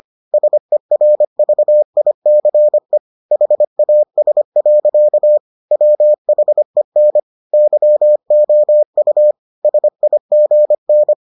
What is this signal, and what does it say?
Morse code, heard as a keyed tone, about 25 wpm, 600 Hz. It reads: SERVICE HAS. WHEN YOU SIGN